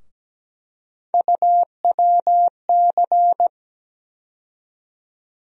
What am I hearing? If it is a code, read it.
Morse code, heard as a keyed tone, about 17 wpm, 700 Hz. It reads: UWC
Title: USA Call Sign Postfixes - 3 Letters 17wpm